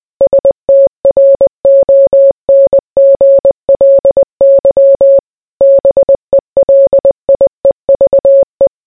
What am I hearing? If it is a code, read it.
Morse code, heard as a keyed tone, about 20 wpm, 550 Hz. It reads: STRONGLY BELIEVE